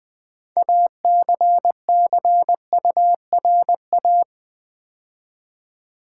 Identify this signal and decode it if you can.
Morse code, heard as a keyed tone, about 20 wpm, 700 Hz. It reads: ACCURA
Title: Build A Word - 3 Letters At A Time - 8-Letter Words - 20wpm